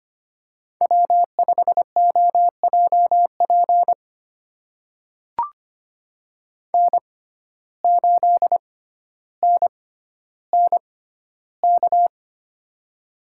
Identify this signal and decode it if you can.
Morse code, heard as a keyed tone, about 25 wpm, spaced as for 13 wpm, 700 Hz. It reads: W5OJP N8NNK
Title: USA Call Signs - 5 Characters - 6x Character Spacing - 25wpm